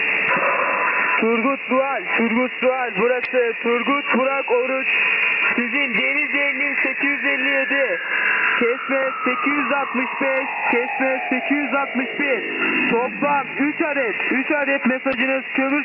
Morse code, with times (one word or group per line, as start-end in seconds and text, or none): none